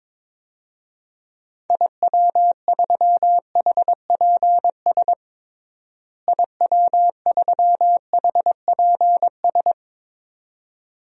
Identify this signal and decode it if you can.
Morse code, heard as a keyed tone, about 22 wpm, 700 Hz. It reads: IW3HPS IW3HPS
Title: DXCC-100 Call Signs - 6 Characters - Repeated 2x - 22wpm